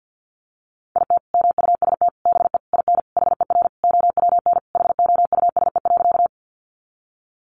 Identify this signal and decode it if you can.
Morse code, heard as a keyed tone, about 48 wpm, 700 Hz. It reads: IT MUST BE IN HER OWN HOUSE.
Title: Sentences from Top 300 Words 50wpm